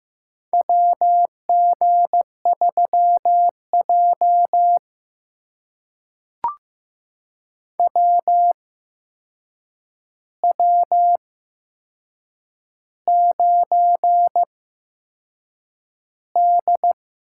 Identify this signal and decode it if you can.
Morse code, heard as a keyed tone, about 15 wpm, spaced as for 6 wpm, 700 Hz. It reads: WG3J WW9D
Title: USA Call Signs - 4 Characters - 8x Character Spacing - 15wpm